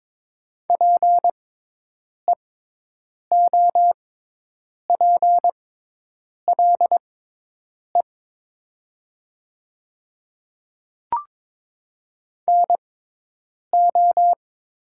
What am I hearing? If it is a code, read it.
Morse code, heard as a keyed tone, about 22 wpm, spaced as for 8 wpm, 700 Hz. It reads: PEOPLE NO
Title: 6x Character Spacing - Top 100 Words - 22wpm